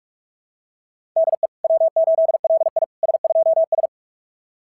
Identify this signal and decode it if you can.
Morse code, heard as a keyed tone, about 46 wpm, 690 Hz. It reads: DE W8LI S2S
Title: SOTA QSO Practice 45wpm